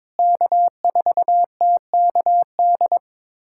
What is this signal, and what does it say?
Morse code, heard as a keyed tone, about 22 wpm, 700 Hz. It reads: K4TKD
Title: USA Callsign DE Callsign - 5 Characters - 22wpm